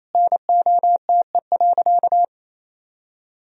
Morse code, heard as a keyed tone, about 28 wpm, 700 Hz. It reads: NOTE.
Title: Sentences from Top 1000 Words - Review Part 2 - 28wpm